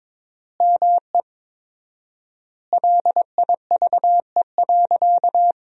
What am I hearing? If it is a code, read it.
Morse code, heard as a keyed tone, about 21 wpm, 700 Hz. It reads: ME LIVE.
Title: 4x Word-Spacing - Sentences from Top 300 Words - 22wpm